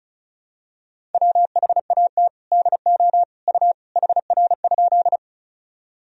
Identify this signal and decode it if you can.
Morse code, heard as a keyed tone, about 36 wpm, 700 Hz. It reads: WHAT DO U HR?
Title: Sentences from Top 300 Words and Abbreviations 35wpm